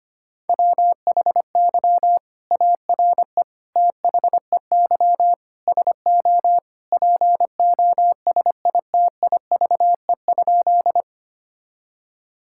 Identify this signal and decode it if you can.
Morse code, heard as a keyed tone, about 25 wpm, 700 Hz. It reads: WHY ARE THEY SO POSITIVE?
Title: Sentences from Top 1000 Words - Review Part 5 - 25wpm